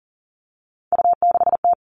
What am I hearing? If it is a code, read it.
Morse code, heard as a keyed tone, about 40 wpm, 690 Hz. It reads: U6T
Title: Three Challenging Letter-Number Combinations 40wpm